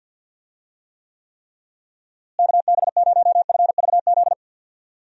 Morse code, heard as a keyed tone, about 50 wpm, 700 Hz. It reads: KB0FVZ
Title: USA Call Signs - 6 Characters - Repeated 3x - 50wpm